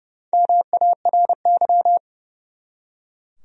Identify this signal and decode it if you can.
Morse code, heard as a keyed tone, about 30 wpm, 700 Hz. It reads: MARY
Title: USA Common Names - Max 4 Letters - 30wpm